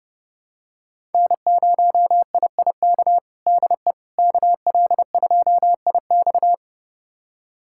Morse code, heard as a keyed tone, about 30 wpm, 700 Hz. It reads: N0IIK DE KL2IX
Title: USA Callsign DE Callsign - 5 Characters - 30wpm